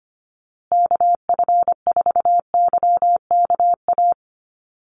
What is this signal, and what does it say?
Morse code, heard as a keyed tone, about 25 wpm, 700 Hz. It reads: KF4YKA
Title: USA Call Signs Repeated 2x 25wpm